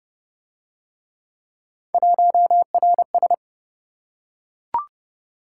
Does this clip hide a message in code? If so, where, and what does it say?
Yes: Morse code from 1.9 to 3.3 s, heard as a keyed tone, about 28 wpm, 700 Hz. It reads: 1RS